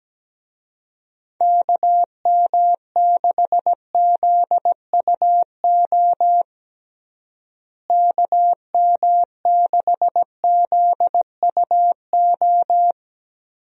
Morse code, heard as a keyed tone, about 17 wpm, 700 Hz. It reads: KM6ZUO KM6ZUO
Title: USA Call Signs - 6 Characters - Repeated 2x - 17wpm